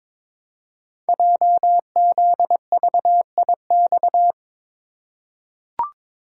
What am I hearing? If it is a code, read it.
Morse code, heard as a keyed tone, about 22 wpm, 700 Hz. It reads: JZVIX
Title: Code Groups - 5 Letters - 22wpm